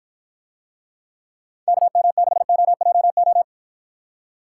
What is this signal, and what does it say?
Morse code, heard as a keyed tone, about 53 wpm, 700 Hz. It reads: KM6OJO